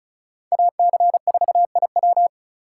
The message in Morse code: AC4IW

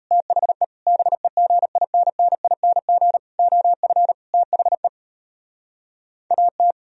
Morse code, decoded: THE BEGINNING OF THE AT

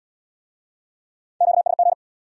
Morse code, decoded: QSG

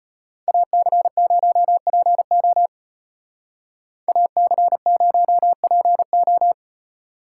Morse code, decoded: AC0PO AC0PO

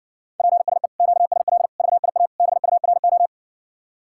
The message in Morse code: PSE QSL VIA BURO

